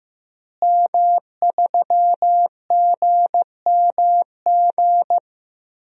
Morse code, heard as a keyed tone, about 15 wpm, 700 Hz. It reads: M3GMG
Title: DXCC-100 Call Signs - 5 Characters - Repeated 3x - 15wpm